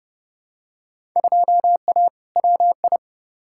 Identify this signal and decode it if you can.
Morse code, heard as a keyed tone, about 30 wpm, 700 Hz. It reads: 2A WI
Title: Field Day Exchanges - 30wpm